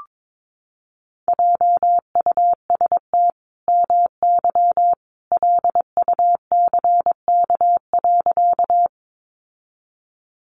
Morse code, heard as a keyed tone, about 22 wpm, 700 Hz. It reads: JUST MY LUCK.
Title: Sentences from Top 1500 Words 22wpm